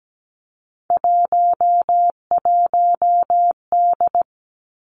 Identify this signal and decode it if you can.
Morse code, heard as a keyed tone, about 17 wpm, 700 Hz. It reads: 11D
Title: Three Challenging Letter-Number Combinations 17wpm